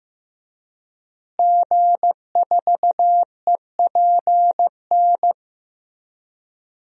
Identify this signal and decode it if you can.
Morse code, heard as a keyed tone, about 16 wpm, 700 Hz. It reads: G4EPN
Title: DXCC-100 Call Signs - 5 Characters - Repeated 2x - 15wpm